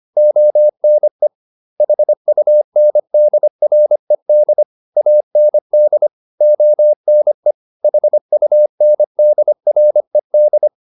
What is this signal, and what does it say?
Morse code, heard as a keyed tone, about 25 wpm, 600 Hz. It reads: ONE HUNDRED AND ONE HUNDRED
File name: the old man and the sea part 1 25wpm